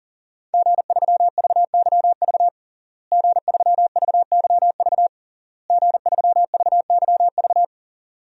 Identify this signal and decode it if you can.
Morse code, heard as a keyed tone, about 40 wpm, 700 Hz. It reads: G3VYV G3VYV G3VYV
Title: DXCC-100 Call Signs - 5 Characters - Repeated 3x - 40wpm